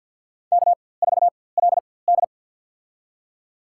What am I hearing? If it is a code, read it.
Morse code, heard as a keyed tone, about 50 wpm, spaced as for 23 wpm, 700 Hz. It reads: K4LD
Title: USA Call Signs - 4x Character Spacing - 50wpm